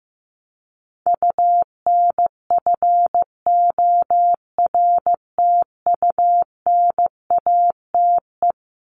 Morse code, heard as a keyed tone, about 14 wpm, 700 Hz. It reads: UNFORTUNATE
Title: LY Suffix - Top 100 Words with Base Words 15wpm